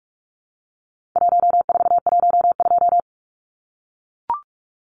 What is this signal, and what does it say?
Morse code, heard as a keyed tone, about 44 wpm, 700 Hz. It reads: 1412
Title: Numbers - 4-Digit 45wpm